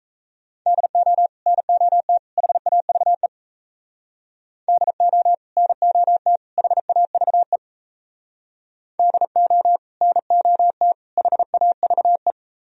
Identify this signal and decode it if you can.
Morse code, heard as a keyed tone, about 32 wpm, 700 Hz. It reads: DO NOT HAVE DO NOT HAVE DO NOT HAVE
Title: Speed-Racing - Sets of 3 Words 28wpm